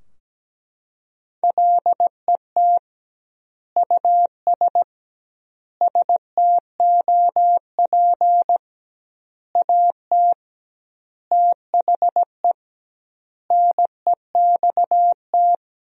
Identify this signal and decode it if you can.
Morse code, heard as a keyed tone, about 17 wpm, 700 Hz. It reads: LET US STOP AT THE NEXT
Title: 2x Word-Spacing - Sentences from Top 400 Words - 17wpm